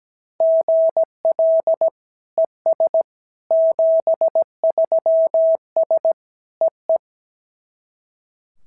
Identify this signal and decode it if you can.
Morse code, heard as a keyed tone, about 17 wpm, 690 Hz. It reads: GL ES 73S EE
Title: SOTA QSO Practice 17wpm